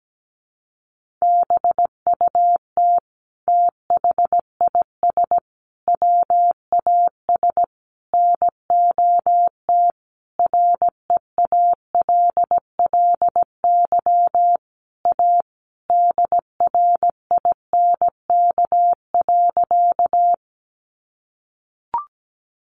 Morse code, heard as a keyed tone, about 17 wpm, 700 Hz. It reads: BUT THIS WAS NOT REALLY A DRINK.
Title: Sentences from Top 1500 Words 17wpm